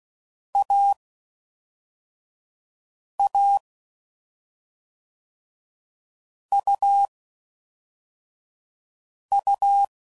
Morse code, heard as a keyed tone, about 16 wpm, spaced as for 4 wpm, 800 Hz. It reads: AA UU
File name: Mixed / Short Runs / AUV AUV4